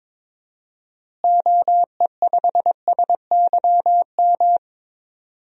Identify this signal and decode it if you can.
Morse code, heard as a keyed tone, about 21 wpm, 700 Hz. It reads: OE5SYM
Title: DXCC-100 Call Signs - 6 Characters - Repeated 2x - 22wpm